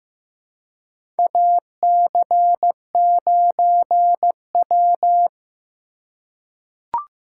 Morse code, heard as a keyed tone, about 15 wpm, 700 Hz. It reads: AC9W
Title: USA Call Signs - 3x Character Spacing - 15wpm